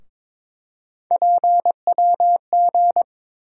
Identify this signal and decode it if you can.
Morse code, heard as a keyed tone, about 22 wpm, 700 Hz. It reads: PWG
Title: USA Call Sign Postfixes - 3 Letters 22wpm